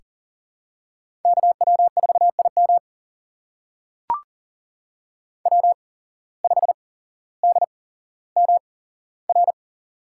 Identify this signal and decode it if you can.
Morse code, heard as a keyed tone, about 40 wpm, spaced as for 17 wpm, 700 Hz. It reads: KW4IM W5DMR
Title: USA Call Signs - 5 Characters - 8x Character Spacing - 40wpm